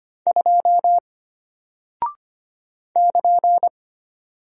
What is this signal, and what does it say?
Morse code, heard as a keyed tone, about 25 wpm, 700 Hz. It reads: 2 <KN>
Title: Single Character (Letter:Number:Punctuation:Prosign) Alt 25wpm